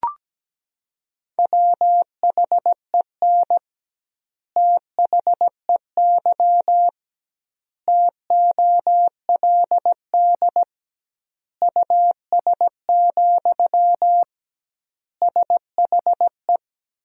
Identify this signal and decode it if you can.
Morse code, heard as a keyed tone, about 17 wpm, 700 Hz. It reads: WHEN THEY TOLD US, SHE
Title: 2x Word-Spacing - Sentences from Top 400 Words - 17wpm